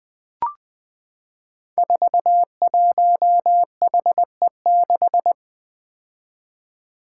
Morse code, heard as a keyed tone, about 20 wpm, 700 Hz. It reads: 41HE6
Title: Five Challenging Letter-Number Combinations 20wpm